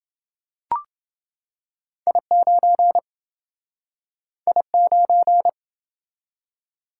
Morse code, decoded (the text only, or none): I9 I9